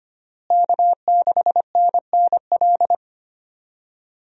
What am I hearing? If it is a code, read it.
Morse code, heard as a keyed tone, about 25 wpm, 700 Hz. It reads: K6NNL
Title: USA Call Signs - 5 Characters - Repeated 3x - 25wpm